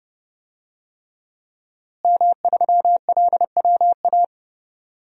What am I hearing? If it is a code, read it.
Morse code, heard as a keyed tone, about 29 wpm, 700 Hz. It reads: M3LWA
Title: DXCC-100 Call Signs - 5 Characters - Repeated 3x - 30wpm